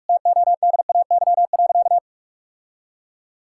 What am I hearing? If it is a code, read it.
Morse code, heard as a keyed tone, about 41 wpm, 700 Hz. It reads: TODAY.
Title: Sentences from Top 1000 Words - Review Part 6 - 45wpm